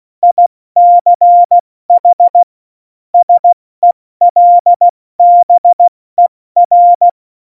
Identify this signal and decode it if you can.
Morse code, heard as a keyed tone, about 16 wpm, 700 Hz. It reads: ICH SELBER
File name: Morsekurs / Witze / Witz 08. Dezember 2020, 70 F80